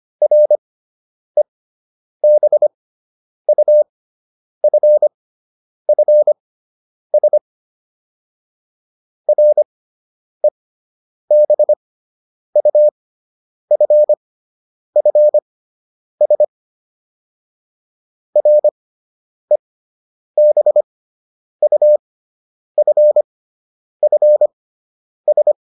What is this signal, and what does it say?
Morse code, heard as a keyed tone, about 25 wpm, spaced as for 9 wpm, 600 Hz. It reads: REBUFFS REBUFFS REBUFFS